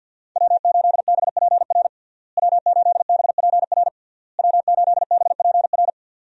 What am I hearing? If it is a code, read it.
Morse code, heard as a keyed tone, about 46 wpm, 700 Hz. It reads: W8BPR W8BPR W8BPR
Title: USA Call Signs - 5 Characters - Repeated 3x - 50wpm